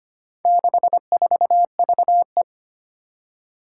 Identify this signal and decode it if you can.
Morse code, heard as a keyed tone, about 25 wpm, 700 Hz. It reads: 64VE
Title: Four Challenging Letter-Number Combinations 25wpm